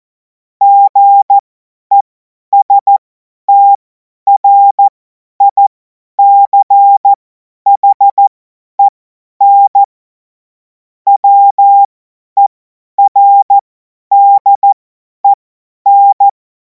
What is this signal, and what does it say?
Morse code, heard as a keyed tone, about 14 wpm, spaced as for 10 wpm, 800 Hz. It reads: GESTRICHEN WERDEN
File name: Morsekurs / Witze / Witz 24. Februar 2026 50 F70